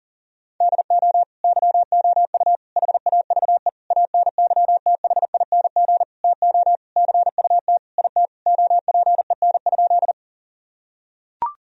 Morse code, heard as a keyed tone, about 40 wpm, 700 Hz. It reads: DO YOU HAVE ANYTHING TO CUT IT OPEN?
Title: Sentences from Top 1000 Words - Review Part 1 - 40wpm